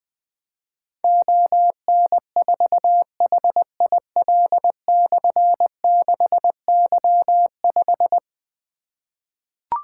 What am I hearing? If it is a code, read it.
Morse code, heard as a keyed tone, about 21 wpm, 700 Hz. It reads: ON4HIL/6Y5
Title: Difficult Call Signs 20wpm